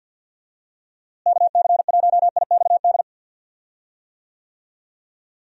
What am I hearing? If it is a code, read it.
Morse code, heard as a keyed tone, about 49 wpm, 700 Hz. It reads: KC1IXD
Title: USA Call Signs - 6 Characters - Repeated 3x - 50wpm